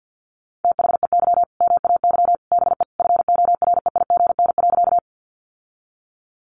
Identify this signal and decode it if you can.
Morse code, heard as a keyed tone, about 50 wpm, 700 Hz. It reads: THEY MAY BE FOREIGN.
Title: Sentences from Top 1400 Words 50wpm